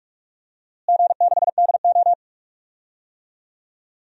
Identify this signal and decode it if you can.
Morse code, heard as a keyed tone, about 42 wpm, 700 Hz. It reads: G6DO
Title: DXCC-100 Call Signs - 4 Characters - Repeated 2x - 45wpm